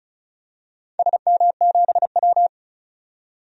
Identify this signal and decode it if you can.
Morse code, heard as a keyed tone, about 36 wpm, 700 Hz. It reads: SM7W
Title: DXCC-100 Call Signs - 4 Characters - 35wpm